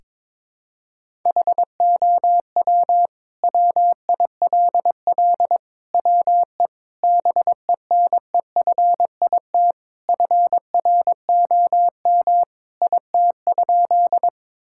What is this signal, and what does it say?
Morse code, heard as a keyed tone, about 22 wpm, 700 Hz. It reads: HOW WILL WE BENEFIT FROM IT?
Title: Sentences from Top 1000 Words - Review Part 6 - 22wpm